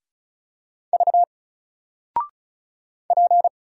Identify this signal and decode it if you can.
Morse code, heard as a keyed tone, about 36 wpm, 700 Hz. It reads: V P
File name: Single-Letter-Number-35wpm